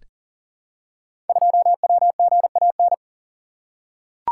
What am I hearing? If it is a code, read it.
Morse code, heard as a keyed tone, about 41 wpm, 700 Hz. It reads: 2WGAN
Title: Code Groups - 5 Letter:Numbers - 40wpm